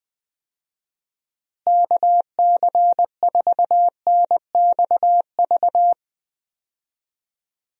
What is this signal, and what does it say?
Morse code, heard as a keyed tone, about 20 wpm, 700 Hz. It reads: KC4NXV
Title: USA Call Signs - 6 Characters - Repeated 3x - 20wpm